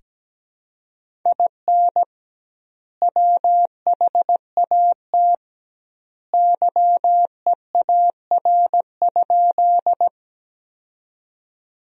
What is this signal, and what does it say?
Morse code, heard as a keyed tone, about 17 wpm, 700 Hz. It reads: IN WHAT YEAR?
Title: 2x Word-Spacing - Sentences from Top 200 Words - 17wpm